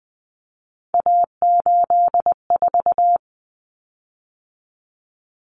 Morse code, heard as a keyed tone, about 20 wpm, 700 Hz. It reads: A84
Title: Common Cut Numbers - 3-Digit 20wpm